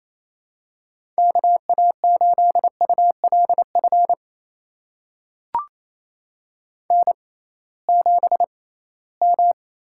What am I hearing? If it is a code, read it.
Morse code, heard as a keyed tone, about 28 wpm, spaced as for 18 wpm, 700 Hz. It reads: KA8ULF N7M
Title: USA Call Signs - 6x Character Spacing - 28wpm